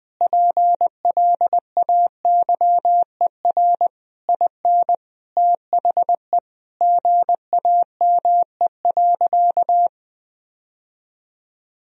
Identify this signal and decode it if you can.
Morse code, heard as a keyed tone, about 21 wpm, 700 Hz. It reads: PLAYER IN THE GAME.
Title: Sentences from Top 1000 Words - Review Part 4 - 20wpm